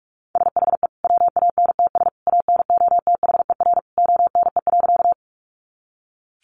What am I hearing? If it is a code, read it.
Morse code, heard as a keyed tone, about 44 wpm, 700 Hz. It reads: SHE WANTS ANOTHER ONE.
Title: Easy Phrases 45wpm